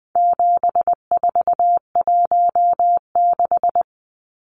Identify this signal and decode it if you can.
Morse code, heard as a keyed tone, about 20 wpm, 700 Hz. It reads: 7416
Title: Numbers - 4-Digit 20wpm